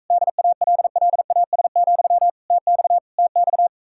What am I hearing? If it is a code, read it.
Morse code, heard as a keyed tone, about 46 wpm, 700 Hz. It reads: DALLAS, TX TX